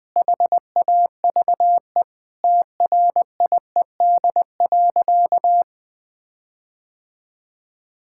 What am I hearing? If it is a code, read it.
Morse code, heard as a keyed tone, about 20 wpm, 700 Hz. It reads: HAVE TRIED.
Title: Sentences with 6-Word Phrases - 20wpm